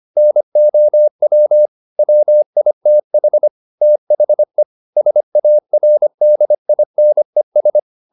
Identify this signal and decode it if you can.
Morse code, heard as a keyed tone, about 25 wpm, 600 Hz. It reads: NOW WITH THE SARDINES
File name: the old man and the sea part 1 25wpm